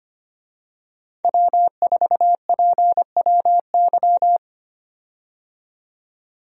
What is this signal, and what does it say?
Morse code, heard as a keyed tone, about 25 wpm, 700 Hz. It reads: W4PWY
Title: USA Call Signs - 5 Characters - 25wpm